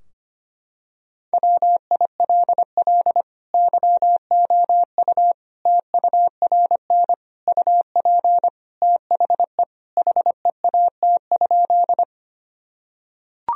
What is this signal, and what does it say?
Morse code, heard as a keyed tone, about 25 wpm, 700 Hz. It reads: WILL YOU TURN UP THE HEAT?
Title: Sentences from Top 1000 Words - Review Part 3 - 25wpm